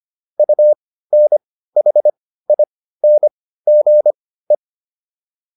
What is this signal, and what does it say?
Morse code, heard as a keyed tone, about 25 wpm, spaced as for 15 wpm, 600 Hz. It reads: UNHINGE